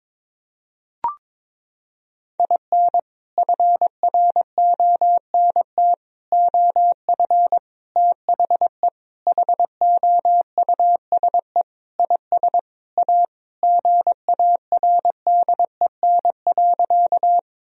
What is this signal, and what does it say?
Morse code, heard as a keyed tone, about 22 wpm, 700 Hz. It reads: IN FRONT OF THE HOUSE IS A GARDEN.
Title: Sentences from Top 1000 Words - Review Part 4 - 22wpm